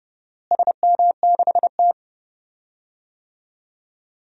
Morse code, heard as a keyed tone, about 31 wpm, 700 Hz. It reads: SM6T
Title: DXCC-100 Call Signs - 4 Characters - Repeated 3x - 30wpm